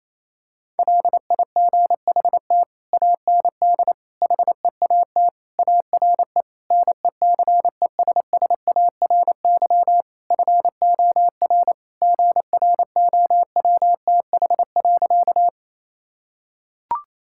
Morse code, heard as a keyed tone, about 28 wpm, 700 Hz. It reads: LIGHT AND HEAT ARE NECESSARY FOR GROWTH.
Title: Sentences from Top 1000 Words - Review Part 5 - 28wpm